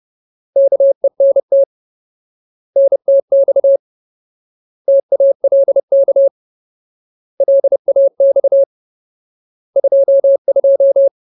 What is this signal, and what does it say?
Morse code, heard as a keyed tone, about 30 wpm, 550 Hz. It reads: KENT NTX TALK LAX 22
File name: Ultimate Word List / 30WPM Ultimate